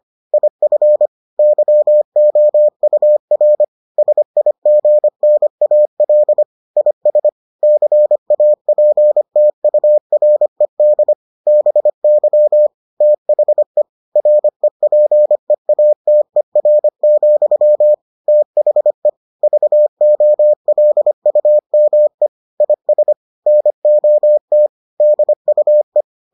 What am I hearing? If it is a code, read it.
Morse code, heard as a keyed tone, about 25 wpm, 600 Hz. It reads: IF YOUR SIGNAL IS CAPTURED BY THE REPEATER, THE VOLUME IS NOT DUE